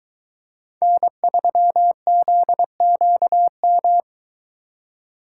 Morse code, heard as a keyed tone, about 22 wpm, 700 Hz. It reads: N3ZQM